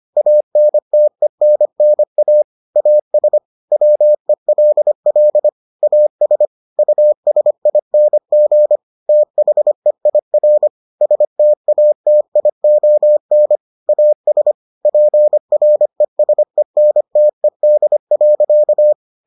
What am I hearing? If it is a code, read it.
Morse code, heard as a keyed tone, about 25 wpm, 600 Hz. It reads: ANTENNA AS WELL AS USING THEIR STATION AS PRESENTED.